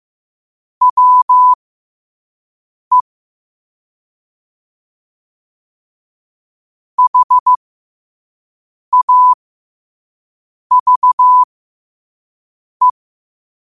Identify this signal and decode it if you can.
Morse code, heard as a keyed tone, about 15 wpm, spaced as for 5 wpm, 1000 Hz. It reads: WE HAVE
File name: qotdfiles / 5wpm